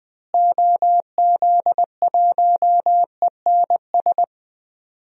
Morse code, heard as a keyed tone, about 20 wpm, 700 Hz. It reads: OZ1ENS
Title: DXCC-100 Call Signs - 6 Characters - Repeated 2x - 20wpm